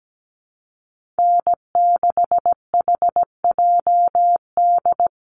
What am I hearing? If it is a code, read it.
Morse code, heard as a keyed tone, about 17 wpm, 700 Hz. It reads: N6HJD
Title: USA Call Signs Repeated 3x 17wpm